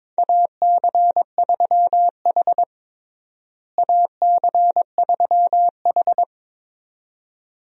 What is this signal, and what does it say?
Morse code, heard as a keyed tone, about 22 wpm, 700 Hz. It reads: AC3H AC3H
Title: USA Call Signs - 4 Characters - Repeated 2x - 22wpm